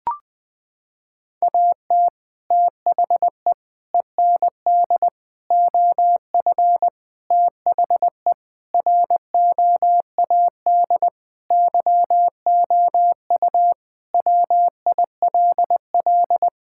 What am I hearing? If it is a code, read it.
Morse code, heard as a keyed tone, about 20 wpm, 700 Hz. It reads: AT THE END OF THE ROAD YOU WILL
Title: Sentences with 3-Word Phrases - 20wpm